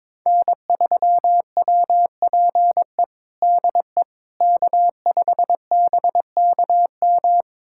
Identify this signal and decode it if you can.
Morse code, heard as a keyed tone, about 21 wpm, 700 Hz. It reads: N3WPE DE K5BKM
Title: USA Callsign DE Callsign - 5 Characters - 22wpm